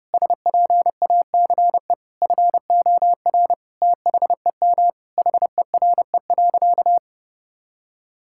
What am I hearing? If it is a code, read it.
Morse code, heard as a keyed tone, about 32 wpm, 700 Hz. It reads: SPACE FOR THEM HERE.
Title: Sentences from Top 1000 Words - Review Part 5 - 30wpm